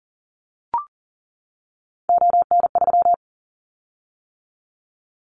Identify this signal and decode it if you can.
Morse code, heard as a keyed tone, about 40 wpm, 700 Hz. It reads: ON3
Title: Callsign Prefixes for Top 100 DXCC - 40wpm